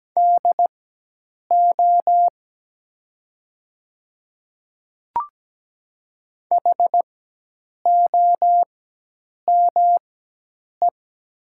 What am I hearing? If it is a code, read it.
Morse code, heard as a keyed tone, about 17 wpm, spaced as for 8 wpm, 700 Hz. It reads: DO HOME
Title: 4x Character Spacing - Top 200 Words - 17wpm